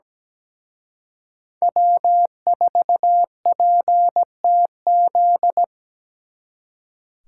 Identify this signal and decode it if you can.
Morse code, heard as a keyed tone, about 20 wpm, 700 Hz. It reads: W4PTZ
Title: Speed Racing - USA Call Signs 17wpm